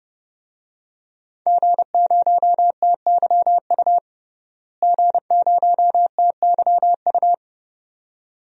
Morse code, heard as a keyed tone, about 32 wpm, 700 Hz. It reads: G0TYU G0TYU
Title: DXCC-100 Call Signs - 5 Characters - Repeated 2x - 30wpm